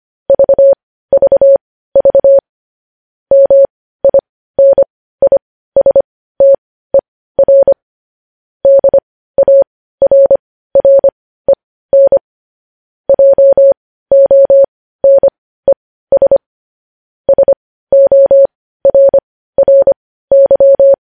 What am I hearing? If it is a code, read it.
Morse code, heard as a keyed tone, about 25 wpm, spaced as for 15 wpm, 550 Hz. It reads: VVV MINISTER DARREN JONES SORRY
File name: podcast / News headlines-1x-25x15